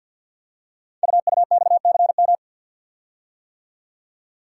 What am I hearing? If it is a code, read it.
Morse code, heard as a keyed tone, about 50 wpm, 690 Hz. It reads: UUXCM
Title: Code Groups - 5 Letters - 50wpm